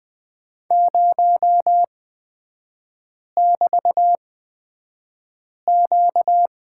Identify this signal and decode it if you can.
Morse code, heard as a keyed tone, about 20 wpm, spaced as for 5 wpm, 700 Hz. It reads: 0<BT>Q
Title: 236 Lesson 38e - Letter q - Three Characters 20wpm